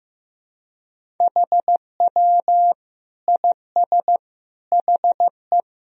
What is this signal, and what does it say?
Morse code, heard as a keyed tone, about 16 wpm, 700 Hz. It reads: HW IS HE
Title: Sentences from Top 100 Words and Abbreviations 15wpm